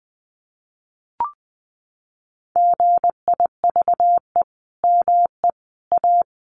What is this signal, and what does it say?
Morse code, heard as a keyed tone, about 20 wpm, 700 Hz. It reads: GIVE ME A
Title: Sentences from Top 1400 Words 20wpm